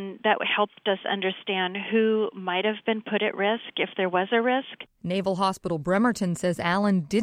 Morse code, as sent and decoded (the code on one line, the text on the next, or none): none